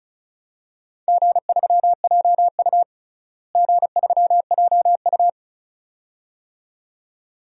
Morse code, heard as a keyed tone, about 34 wpm, 700 Hz. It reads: G3JU G3JU
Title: DXCC-100 Call Signs - 4 Characters - Repeated 2x - 35wpm